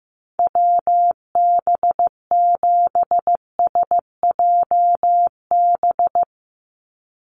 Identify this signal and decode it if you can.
Morse code, heard as a keyed tone, about 15 wpm, 700 Hz. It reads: WB7SJB
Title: USA Call Signs 15wpm